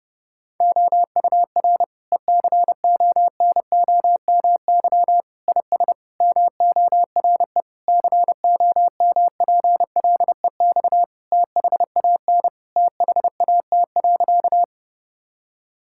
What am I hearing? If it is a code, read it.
Morse code, heard as a keyed tone, about 30 wpm, 700 Hz. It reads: OUR ECONOMY IS MORE COMPLEX THAN THAT.
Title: Sentences from Top 1000 Words - Review Part 6 - 30wpm